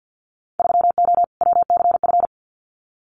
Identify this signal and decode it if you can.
Morse code, heard as a keyed tone, about 50 wpm, 700 Hz. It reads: 3O WCF
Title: Winter Field Day Exchanges Encore 50wpm